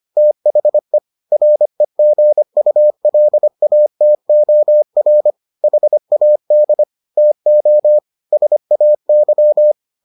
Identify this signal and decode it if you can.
Morse code, heard as a keyed tone, about 25 wpm, 600 Hz. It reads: THE REGULATOR HAD TO SAY